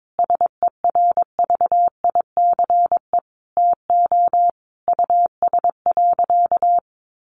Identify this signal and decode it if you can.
Morse code, heard as a keyed tone, about 22 wpm, 700 Hz. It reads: SERVICE TO US.
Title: Sentences from Top 1200 Words 22wpm